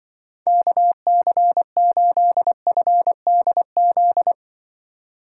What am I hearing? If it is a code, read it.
Morse code, heard as a keyed tone, about 24 wpm, 700 Hz. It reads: KC8FDZ
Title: Speed Racing - USA 2x3 Call Signs 20wpm